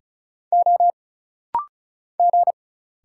Morse code, heard as a keyed tone, about 34 wpm, 700 Hz. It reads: O G
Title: Single Letters - Rapid-Fire 35wpm